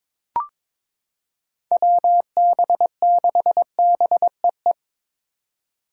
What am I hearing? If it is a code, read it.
Morse code, heard as a keyed tone, about 22 wpm, 700 Hz. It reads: WB6BEE
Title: K1USN SST QSO Practice - 22wpm